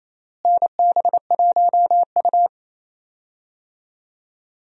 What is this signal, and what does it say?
Morse code, heard as a keyed tone, about 28 wpm, 700 Hz. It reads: NB1U
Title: USA Call Signs - 4 Characters - Repeated 2x - 28wpm